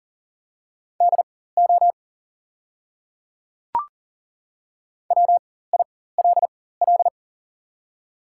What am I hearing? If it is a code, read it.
Morse code, heard as a keyed tone, about 40 wpm, spaced as for 19 wpm, 700 Hz. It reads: DO WILL
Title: 4x Character Spacing - Top 200 Words - 40wpm